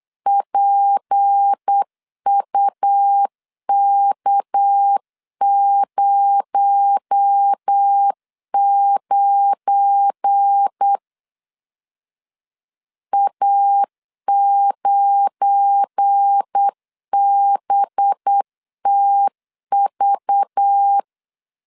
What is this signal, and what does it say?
Morse code, heard as a keyed tone, about 8 wpm, 800 Hz. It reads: PUK09 A9BTV
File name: LEZIONI / 7 GF9Z / 7 GF9 8wpm 3545